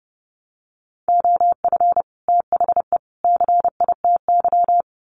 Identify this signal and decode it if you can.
Morse code, heard as a keyed tone, about 30 wpm, 700 Hz. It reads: OF THE CITY
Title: Sets of 3 Words 30wpm